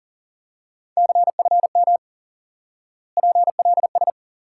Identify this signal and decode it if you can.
Morse code, heard as a keyed tone, about 40 wpm, 700 Hz. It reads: CFM PLS
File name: IQE Course 3 - Lesson 89 - YAGI - Two Elements - 40wpm